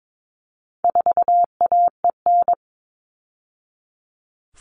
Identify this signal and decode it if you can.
Morse code, heard as a keyed tone, about 22 wpm, 700 Hz. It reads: 4AEN
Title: All Cut Numbers - 4-Digit 22wpm